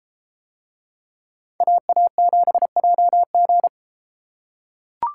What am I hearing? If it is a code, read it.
Morse code, heard as a keyed tone, about 31 wpm, 700 Hz. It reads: AA7JG